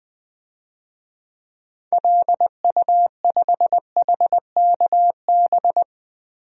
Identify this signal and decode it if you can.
Morse code, heard as a keyed tone, about 21 wpm, 700 Hz. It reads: LU5HKB
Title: DXCC-100 Call Signs - 6 Characters - Repeated 3x - 20wpm